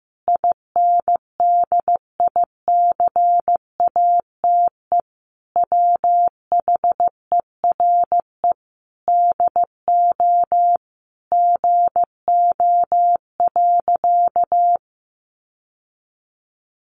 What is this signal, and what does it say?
Morse code, heard as a keyed tone, about 15 wpm, 700 Hz. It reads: INDICATE WHERE DO GO.
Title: Sentences from Top 1300 Words 15wpm